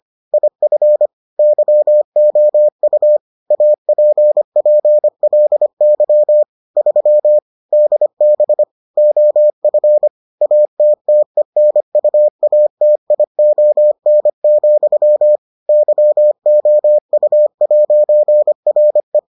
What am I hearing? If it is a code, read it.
Morse code, heard as a keyed tone, about 25 wpm, 600 Hz. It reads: IF YOU APPLY 3 DB OF ATTENUATION, YOU'RE